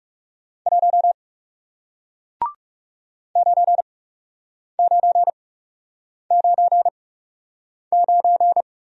Morse code, heard as a keyed tone, about 39 wpm, spaced as for 8 wpm, 700 Hz. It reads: 1 9999